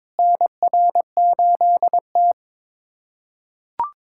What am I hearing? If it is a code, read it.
Morse code, heard as a keyed tone, about 21 wpm, 700 Hz. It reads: NR8T
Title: USA Callsign DE Callsign - 4 Characters - 22wpm